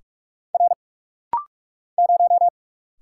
Morse code, heard as a keyed tone, about 46 wpm, 700 Hz. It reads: R 0
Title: Single Letter-Number Alt - Rapid-Fire 45wpm